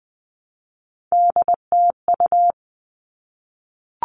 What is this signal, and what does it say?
Morse code, heard as a keyed tone, about 20 wpm, 700 Hz. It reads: DTU
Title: All Cut Numbers - 3-Digit 20wpm